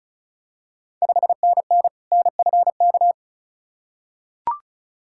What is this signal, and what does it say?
Morse code, heard as a keyed tone, about 34 wpm, 690 Hz. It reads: 5NN NFK
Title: Ontario QSO Party Exchange 35wpm